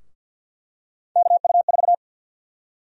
Morse code, heard as a keyed tone, about 48 wpm, 700 Hz. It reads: KU4
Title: USA Call Sign Prefixes - Repeated 2x - 50wpm